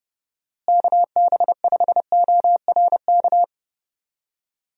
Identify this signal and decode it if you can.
Morse code, heard as a keyed tone, about 30 wpm, 700 Hz. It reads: KB5ORK
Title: USA Call Signs - 6 Characters - Repeated 2x - 30wpm